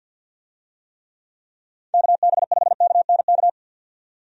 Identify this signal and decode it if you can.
Morse code, heard as a keyed tone, about 51 wpm, 700 Hz. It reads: KB5KNK
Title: USA Call Signs - 6 Characters - 50wpm